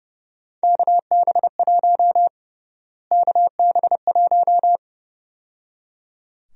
Morse code, heard as a keyed tone, about 31 wpm, 700 Hz. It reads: KB1 KB1
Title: USA Call Sign Prefixes - Repeated 2x - 30wpm